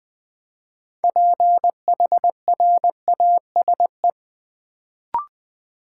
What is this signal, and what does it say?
Morse code, heard as a keyed tone, about 20 wpm, 700 Hz. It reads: PHRASE
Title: Build A Word - 2 Letters At A Time - 6-Letter Words - 20wpm